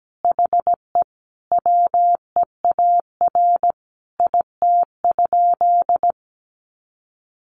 Morse code, heard as a keyed tone, about 17 wpm, 700 Hz. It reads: HE WEAR IT?
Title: Sentences from Top 1500 Words 17wpm